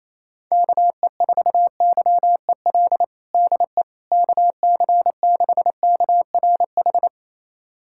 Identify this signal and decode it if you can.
Morse code, heard as a keyed tone, about 28 wpm, 700 Hz. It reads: KE4YEL DE KC6KRH
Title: USA Callsign DE Callsign 28wpm